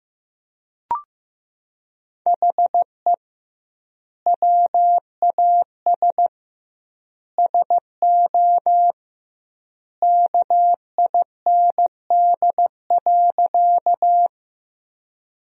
Morse code, heard as a keyed tone, about 15 wpm, 700 Hz. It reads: HE WAS SO KIND.
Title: 2x Word-Spacing - Sentences from Top 200 Words - 15wpm